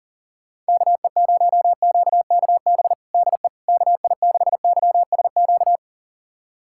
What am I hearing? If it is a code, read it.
Morse code, heard as a keyed tone, about 40 wpm, 700 Hz. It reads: KE0QKB DE KI6YSQ
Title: USA Callsign DE Callsign - 6 Characters - 40wpm